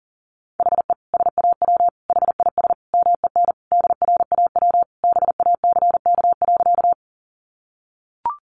Morse code, heard as a keyed tone, about 40 wpm, 700 Hz. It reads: HE SAW HIS MEN DRAW BACK.
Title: Sentences from Top 300 Words 40wpm